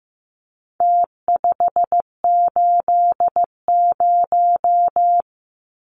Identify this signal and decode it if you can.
Morse code, heard as a keyed tone, about 15 wpm, 700 Hz. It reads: T580
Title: Numbers - Contest Style - 4-Digit 15wpm